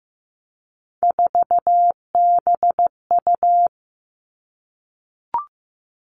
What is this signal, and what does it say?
Morse code, heard as a keyed tone, about 15 wpm, 700 Hz. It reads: 4BU
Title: Three Challenging Letter-Number Combinations 15wpm